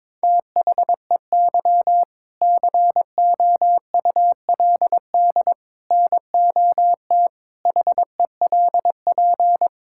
Morse code, heard as a keyed tone, about 22 wpm, 700 Hz. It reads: THEY COULD NOT HELP
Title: Sentences from Top 1000 Words - Review Part 5 - 22wpm